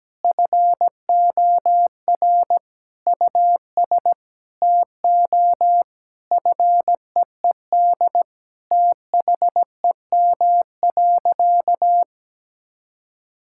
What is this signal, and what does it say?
Morse code, heard as a keyed tone, about 17 wpm, 700 Hz. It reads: FOR US TO FEED THEM.
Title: Sentences from Top 1600 Words 17wpm